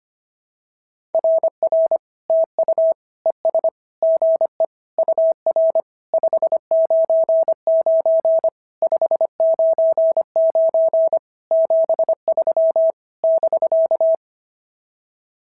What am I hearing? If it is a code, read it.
Morse code, heard as a keyed tone, about 25 wpm, 650 Hz. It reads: RR TU ES GE UR 599 599 73 <BK>
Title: SOTA QSO Practice 25wpm